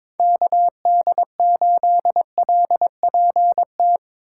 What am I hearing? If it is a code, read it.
Morse code, heard as a keyed tone, about 22 wpm, 700 Hz. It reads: KD8LPT
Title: USA Call Signs - 6 Characters - 4x Character Spacing - 22wpm